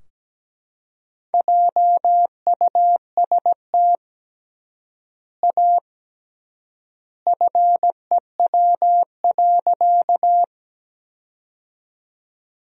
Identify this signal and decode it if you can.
Morse code, heard as a keyed tone, about 17 wpm, 700 Hz. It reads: JUST A FEW.
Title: 3x Word-Spacing - Sentences from Top 300 Words - 17wpm